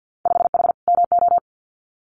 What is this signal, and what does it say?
Morse code, heard as a keyed tone, about 50 wpm, 700 Hz. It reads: 5H MO
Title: Winter Field Day Exchanges Encore 50wpm